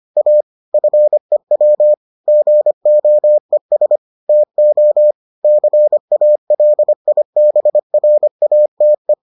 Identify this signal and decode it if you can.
Morse code, heard as a keyed tone, about 25 wpm, 600 Hz. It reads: A FEW GOES TO CALIBRATE